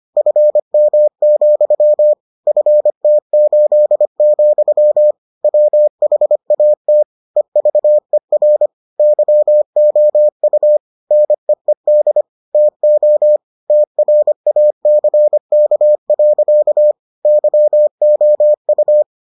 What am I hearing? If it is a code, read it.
Morse code, heard as a keyed tone, about 25 wpm, 600 Hz. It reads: FM, FT8, WHAT EVER YOU NEED TO TRACK. YOU